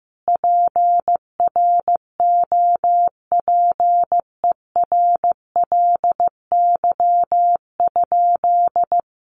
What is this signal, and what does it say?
Morse code, heard as a keyed tone, about 15 wpm, 700 Hz. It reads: PROPERLY?
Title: Sentences from Top 1100 Words 15wpm